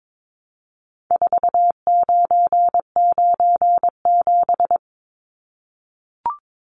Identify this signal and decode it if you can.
Morse code, heard as a keyed tone, about 22 wpm, 700 Hz. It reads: 4997
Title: Numbers - 4-Digit 22wpm